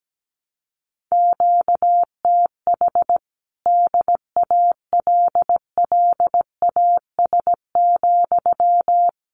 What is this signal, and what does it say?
Morse code, heard as a keyed tone, about 17 wpm, 700 Hz. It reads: QTH DALLAS,
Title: Common QSO Phrases Encore 17wpm